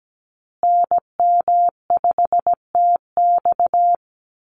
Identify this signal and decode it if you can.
Morse code, heard as a keyed tone, about 17 wpm, 700 Hz. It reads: NM5TX
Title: USA Call Signs Repeated 2x 17wpm